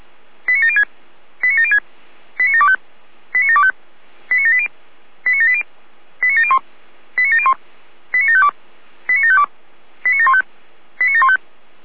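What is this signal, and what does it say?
Five-tone selective call (ZVEI), heard as quick runs of five tones, tones of about 70 ms. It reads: 89706 89706 89725 89725 89780 89780 89701 89701 89752 89752 89715 89715
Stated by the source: ZVEI five-tone selective call received by SDR.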